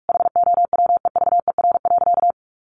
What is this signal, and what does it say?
Morse code, heard as a keyed tone, about 44 wpm, 700 Hz. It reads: HOWEVER.
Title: Sentences from Top 1100 Words 45wpm